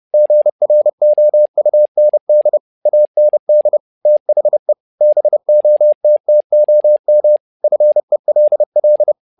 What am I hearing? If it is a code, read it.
Morse code, heard as a keyed tone, about 30 wpm, 600 Hz. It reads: GROUND AND THE BOTTOM FELL
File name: learning center / tower trouble / tower trouble 30wpm